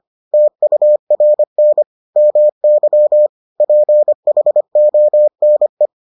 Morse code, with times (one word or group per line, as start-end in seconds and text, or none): 0.3-1.8 TURN
2.2-3.3 MY
3.6-5.9 PHONE